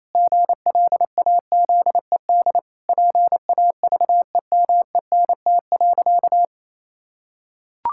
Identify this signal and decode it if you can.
Morse code, heard as a keyed tone, about 28 wpm, 700 Hz. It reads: GLAZED PAVEMENT.
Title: Pangram Challenge v2 - 28wpm